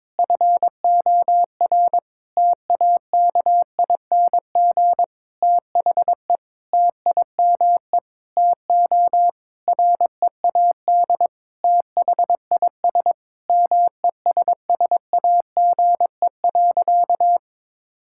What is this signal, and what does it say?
Morse code, heard as a keyed tone, about 22 wpm, 700 Hz. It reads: FOR TAKING THE TIME TO READ THIS MESSAGE.
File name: Sentences with 9-Word Phrases - 22wpm